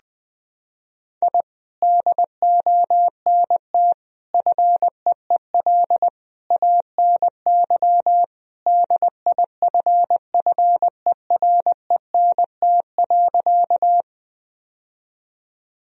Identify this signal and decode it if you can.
Morse code, heard as a keyed tone, about 21 wpm, 700 Hz. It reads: I DONT FEEL ANY DIFFERENT.
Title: Sentences from Top 1000 Words - Review Part 3 - 20wpm